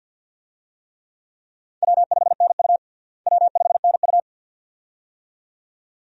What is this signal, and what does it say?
Morse code, heard as a keyed tone, about 53 wpm, 700 Hz. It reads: W5NU W5NU
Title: USA Call Signs - 4 Characters - Repeated 2x - 50wpm